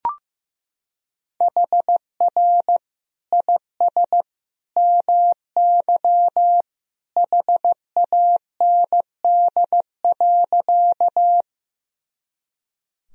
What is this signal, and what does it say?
Morse code, heard as a keyed tone, about 15 wpm, 700 Hz. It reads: HR IS MY HAND.
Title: Sentences from Top 200 Words and Abbreviations 15wpm